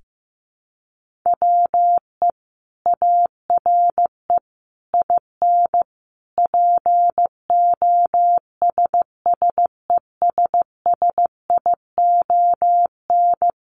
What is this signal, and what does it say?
Morse code, heard as a keyed tone, about 15 wpm, 700 Hz. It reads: WE ARE IN POSSESSION
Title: Sentences from Top 1400 Words 15wpm